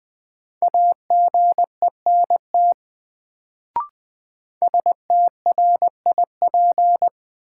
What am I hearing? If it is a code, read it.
Morse code, heard as a keyed tone, about 20 wpm, 700 Hz. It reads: AGENT STRIP
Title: Sentences from Top 1600 Words - New Word Drill - 20wpm